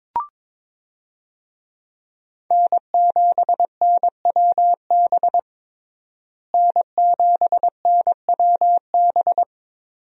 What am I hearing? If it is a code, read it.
Morse code, heard as a keyed tone, about 21 wpm, 700 Hz. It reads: N7NWB N7NWB
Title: USA Call Signs - 5 Characters - Repeated 2x - 22wpm